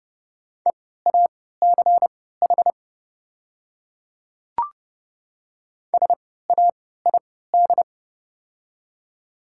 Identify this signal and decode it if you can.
Morse code, heard as a keyed tone, about 30 wpm, spaced as for 17 wpm, 700 Hz. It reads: EACH SAID
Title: 3x Character Spacing - Top 100 Words - 30wpm